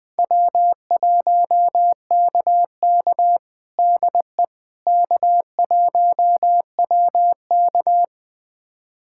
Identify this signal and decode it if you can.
Morse code, heard as a keyed tone, about 21 wpm, 700 Hz. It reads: W1KK DE K1WK
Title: USA Callsign DE Callsign - 4 Characters - 20wpm